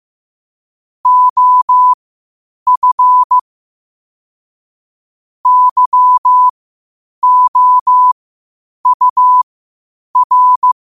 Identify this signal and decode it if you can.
Morse code, heard as a keyed tone, about 15 wpm, spaced as for 8 wpm, 1000 Hz. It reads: OF YOUR
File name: qotdfiles / 8wpm